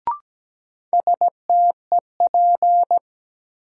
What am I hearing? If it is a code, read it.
Morse code, heard as a keyed tone, about 17 wpm, 700 Hz. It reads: STEP
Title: Sets of 5 Words Encore - New Word Drill - 17wpm